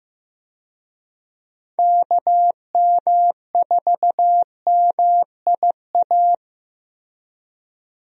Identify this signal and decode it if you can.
Morse code, heard as a keyed tone, about 15 wpm, 700 Hz. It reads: KM4MIA
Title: USA Call Signs - 6 Characters - Repeated 3x - 15wpm